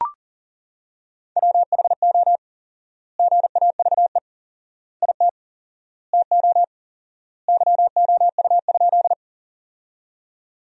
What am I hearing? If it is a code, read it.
Morse code, heard as a keyed tone, about 40 wpm, 700 Hz. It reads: WHO GAVE IT TO YOU?
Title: 4x Word-Spacing - Sentences from Top 500 Words - 40wpm